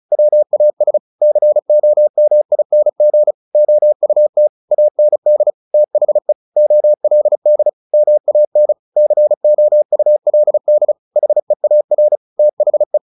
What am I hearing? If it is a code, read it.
Morse code, heard as a keyed tone, about 34 wpm, 600 Hz. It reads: WAS COMING OUT AND THE OLD MAN COULD HEAR THE
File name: the old man and the sea 2 35wpm